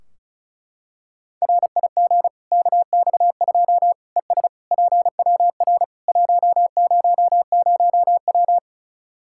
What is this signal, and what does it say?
Morse code, heard as a keyed tone, about 34 wpm, 700 Hz. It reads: RIG KX2 ES PWR 100W
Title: K1USN SST QSO Practice 35wpm